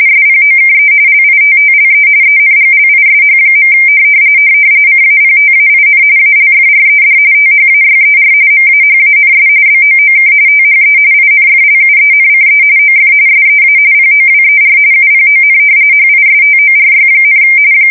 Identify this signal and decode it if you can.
RTTY, about 45.45 baud. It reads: SPOKE LITTLE ENGLISH. CAPTAIN ALFRED THAYER MAHANS BOOK THE INFLUENCE OF SEA POWER UPON HISTORY,
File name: fsk170 / usnavy3